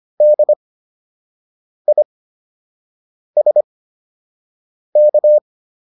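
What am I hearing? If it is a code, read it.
Morse code, heard as a keyed tone, about 25 wpm, spaced as for 6 wpm, 600 Hz. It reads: DISK
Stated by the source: Fundamental class Morse code practice material.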